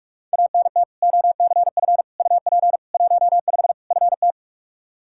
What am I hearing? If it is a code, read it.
Morse code, heard as a keyed tone, about 46 wpm, 700 Hz. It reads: ANT OCF UP 15 FT
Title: Practice QSO - Exchange 2 - Info - 45wpm